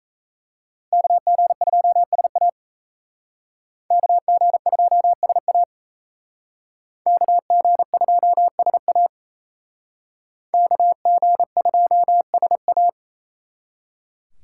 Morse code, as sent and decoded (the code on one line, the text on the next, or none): -.- --. ..--- ... .- / -.- --. ..--- ... .- / -.- --. ..--- ... .- / -.- --. ..--- ... .-
KG2SA KG2SA KG2SA KG2SA